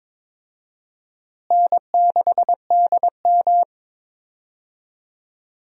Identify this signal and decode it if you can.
Morse code, heard as a keyed tone, about 21 wpm, 700 Hz. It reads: N6DM
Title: USA Call Signs - 4 Characters - Repeated 3x - 22wpm